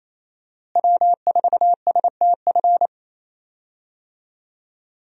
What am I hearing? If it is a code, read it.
Morse code, heard as a keyed tone, about 28 wpm, 700 Hz. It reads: W4STF
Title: USA Call Signs - 5 Characters - Repeated 2x - 28wpm